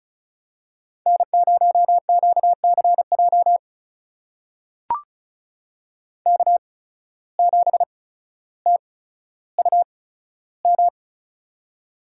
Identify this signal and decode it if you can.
Morse code, heard as a keyed tone, about 36 wpm, spaced as for 15 wpm, 700 Hz. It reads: N0QCJ K7TUM
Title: USA Call Signs - 5 Characters - 8x Character Spacing - 35wpm